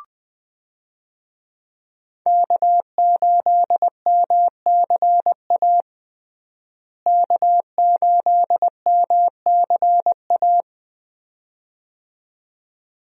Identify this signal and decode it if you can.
Morse code, heard as a keyed tone, about 20 wpm, 700 Hz. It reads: K8MCA K8MCA
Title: USA Call Signs - 5 Characters - Repeated 2x - 20wpm